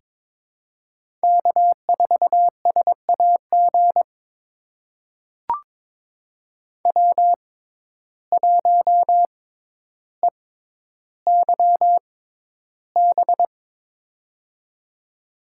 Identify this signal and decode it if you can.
Morse code, heard as a keyed tone, about 22 wpm, spaced as for 11 wpm, 700 Hz. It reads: K4SAG W1EYB
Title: USA Call Signs - 6x Character Spacing - 22wpm